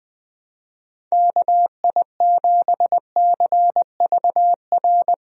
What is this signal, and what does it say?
Morse code, heard as a keyed tone, about 20 wpm, 700 Hz. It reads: KI7CVR
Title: USA Call Signs - 6 Characters - Repeated 2x - 20wpm